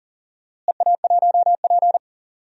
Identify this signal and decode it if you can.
Morse code, heard as a keyed tone, about 38 wpm, 700 Hz. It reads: EA1P
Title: DXCC-100 Call Signs - 4 Characters - Repeated 3x - 40wpm